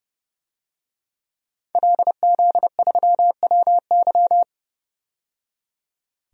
Morse code, heard as a keyed tone, about 30 wpm, 700 Hz. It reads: LZ3WY
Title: DXCC-100 Call Signs - 5 Characters - 30wpm